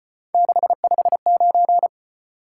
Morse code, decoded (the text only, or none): 659